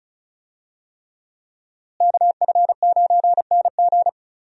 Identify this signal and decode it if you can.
Morse code, heard as a keyed tone, about 36 wpm, 700 Hz. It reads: KF9NG
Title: USA Call Signs - 5 Characters - Repeated 3x - 35wpm